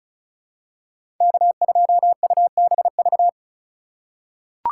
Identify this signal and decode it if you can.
Morse code, heard as a keyed tone, about 34 wpm, 700 Hz. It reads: K2UBV
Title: USA Call Signs - 5 Characters - 2x Character Spacing - 35wpm